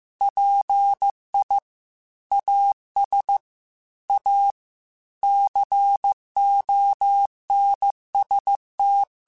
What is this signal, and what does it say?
Morse code, heard as a keyed tone, about 15 wpm, 800 Hz. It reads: PI AS A CONST
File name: Technical Subjects / Technical Subjects-mp3s / XL @13